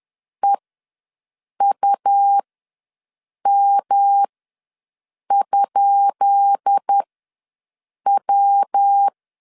Morse code, heard as a keyed tone, about 10 wpm, spaced as for 6 wpm, 800 Hz. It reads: EUM?W
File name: LEZIONI / 10 SIGNS / 10B INT ws 10wpm 4134